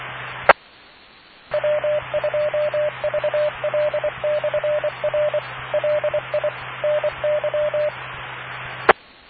Morse code, heard as a keyed tone, about 24 wpm, 600 Hz, with noise. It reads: E W2VL/R LI NY E